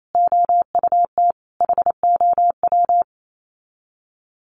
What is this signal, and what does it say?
Morse code, heard as a keyed tone, about 28 wpm, 700 Hz. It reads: OUT HOW
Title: Sets of 5 Words Encore 28wpm